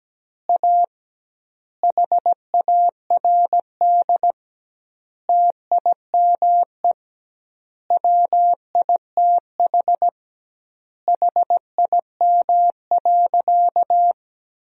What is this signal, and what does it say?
Morse code, heard as a keyed tone, about 17 wpm, 700 Hz. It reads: A HARD TIME WITH HIM.
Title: 2x Word-Spacing - Sentences from Top 300 Words - 17wpm